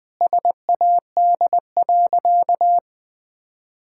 Morse code, heard as a keyed tone, about 21 wpm, 700 Hz. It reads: SAD.
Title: Sentences with 2-Word Phrases - 20wpm